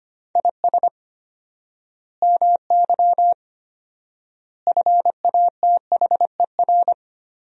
Morse code, heard as a keyed tone, about 25 wpm, 700 Hz. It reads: IS MY FATHER
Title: 4x Word-Spacing - Sentences from Top 300 Words - 25wpm